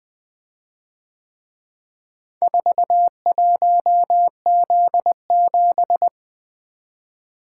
Morse code, heard as a keyed tone, about 23 wpm, 700 Hz. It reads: 41Z7
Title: Speed-Racing - Four Letter-Number Combinations - 20wpm